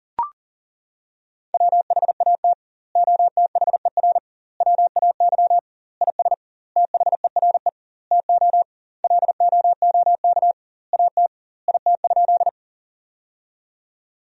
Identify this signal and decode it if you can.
Morse code, heard as a keyed tone, about 40 wpm, 700 Hz. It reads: WHAT OTHER WAY IS THERE TO LOOK AT IT?
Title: 2x Word-Spacing - Sentences from Top 100 Words - 40wpm